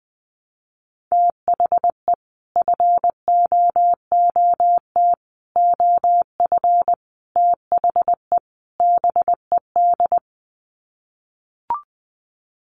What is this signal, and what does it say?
Morse code, heard as a keyed tone, about 20 wpm, 700 Hz. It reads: THE FOOT OF THE BED
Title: Sets of 5 Words Encore2 20wpm